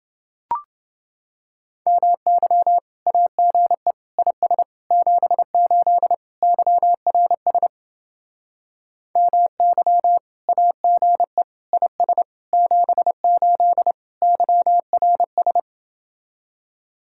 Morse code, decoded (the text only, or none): MY AGE IS 78 YRS MY AGE IS 78 YRS